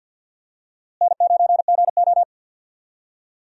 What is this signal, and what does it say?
Morse code, heard as a keyed tone, about 46 wpm, 700 Hz. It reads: N9GO
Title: USA Call Signs - 4 Characters - Repeated 3x - 50wpm